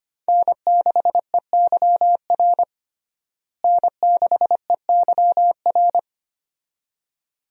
Morse code, heard as a keyed tone, about 25 wpm, 700 Hz. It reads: N6EYR N6EYR
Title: USA Call Signs - 5 Characters - Repeated 2x - 25wpm